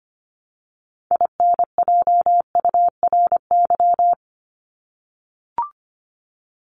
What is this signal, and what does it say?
Morse code, heard as a keyed tone, about 25 wpm, 700 Hz. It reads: INJURY
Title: Top 1751-2000 Words 25wpm